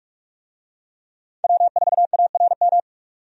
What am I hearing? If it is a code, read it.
Morse code, heard as a keyed tone, about 44 wpm, 700 Hz. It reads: W4ARM
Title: USA Call Signs - 5 Characters - 45wpm